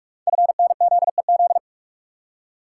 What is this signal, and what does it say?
Morse code, heard as a keyed tone, about 44 wpm, 700 Hz. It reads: FNZEZ
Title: Code Groups - 5 Letters - 45wpm